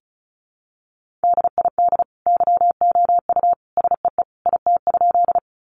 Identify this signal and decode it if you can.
Morse code, heard as a keyed tone, about 34 wpm, 700 Hz. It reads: DID YOU SEE IT?
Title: Sentences from Top 100 Words 35wpm